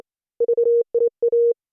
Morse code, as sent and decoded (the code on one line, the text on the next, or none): ...- .. .-
VIA